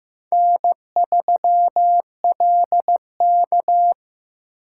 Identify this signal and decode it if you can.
Morse code, heard as a keyed tone, about 18 wpm, 700 Hz. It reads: N3LK
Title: Speed Racing - USA 1x2 Call Signs 15wpm